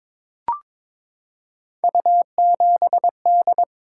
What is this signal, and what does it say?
Morse code, heard as a keyed tone, about 22 wpm, 700 Hz. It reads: U7D
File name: practice / 22wpm / Three Letter-Number Combinations - 22wpm